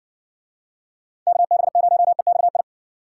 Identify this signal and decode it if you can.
Morse code, heard as a keyed tone, about 62 wpm, 700 Hz. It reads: KB0EXI